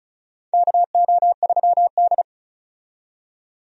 Morse code, heard as a keyed tone, about 34 wpm, 700 Hz. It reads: KO3D
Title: USA Call Signs - 4 Characters - 35wpm